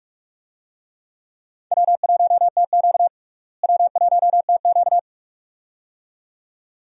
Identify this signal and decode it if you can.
Morse code, heard as a keyed tone, about 42 wpm, 700 Hz. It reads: W1TQ W1TQ
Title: USA Call Signs - 4 Characters - Repeated 2x - 45wpm